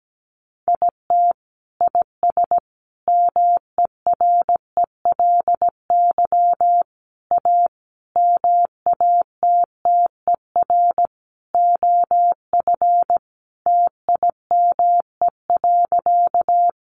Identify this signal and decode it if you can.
Morse code, heard as a keyed tone, about 17 wpm, 700 Hz. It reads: IT IS MERELY A MATTER OF TIME.
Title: Sentences from Top 1400 Words 17wpm